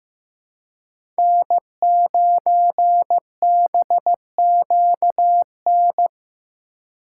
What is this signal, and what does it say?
Morse code, heard as a keyed tone, about 18 wpm, 700 Hz. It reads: N9BQN
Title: Speed Racing - USA 1x3 Call Signs 15wpm